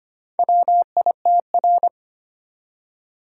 Morse code, heard as a keyed tone, about 25 wpm, 690 Hz. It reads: WITR
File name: practice / 25wpm / Four Letter-Number Combinations - 25wpm